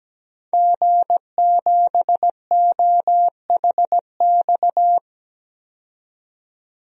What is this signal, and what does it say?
Morse code, heard as a keyed tone, about 17 wpm, 700 Hz. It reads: G7OHX
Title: DXCC-100 Call Signs - 5 Characters - 17wpm